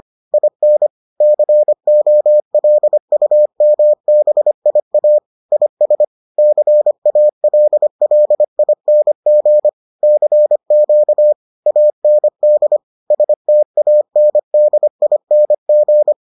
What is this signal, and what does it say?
Morse code, heard as a keyed tone, about 25 wpm, 600 Hz. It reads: IN COLUMBIA IS CALLING CQ AND STANDING